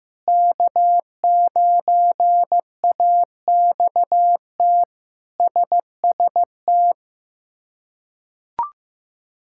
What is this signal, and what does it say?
Morse code, heard as a keyed tone, about 15 wpm, 700 Hz. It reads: K9AXT SST
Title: K1USN SST QSO Practice - 15wpm